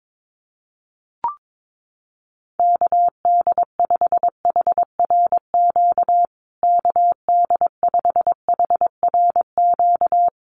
Morse code, heard as a keyed tone, about 22 wpm, 700 Hz. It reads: KD5HRQ KD5HRQ
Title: USA Call Signs Repeated 3x 22wpm